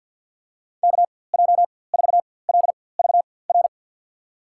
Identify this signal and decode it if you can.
Morse code, heard as a keyed tone, about 53 wpm, spaced as for 24 wpm, 700 Hz. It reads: KJ4LVR
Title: USA Call Signs - 6 Characters - 4x Character Spacing - 50wpm